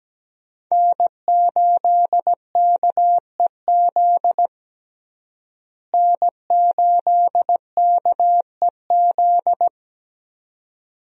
Morse code, heard as a keyed tone, about 17 wpm, 700 Hz. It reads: N8KEZ N8KEZ
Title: USA Call Signs - 5 Characters - Repeated 2x - 17wpm